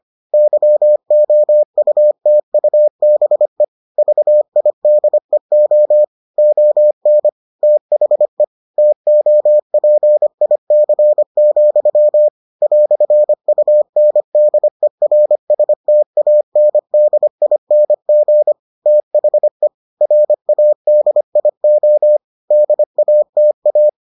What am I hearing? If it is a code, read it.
Morse code, heard as a keyed tone, about 25 wpm, 600 Hz. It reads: YOUTUBE VIDEO ON THE TOPIC, "UNDERSTANDING THE RADIO DATA